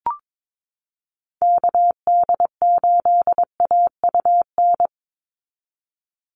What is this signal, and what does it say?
Morse code, heard as a keyed tone, about 22 wpm, 700 Hz. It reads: KD8AUN
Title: USA Call Signs 22wpm